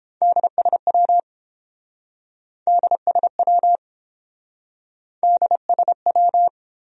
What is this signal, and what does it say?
Morse code, heard as a keyed tone, about 28 wpm, 700 Hz. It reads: DSW DSW DSW